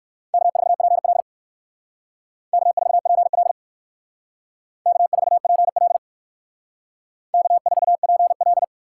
Morse code, heard as a keyed tone, about 60 wpm, 700 Hz. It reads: K4PL K4PL K4PL K4PL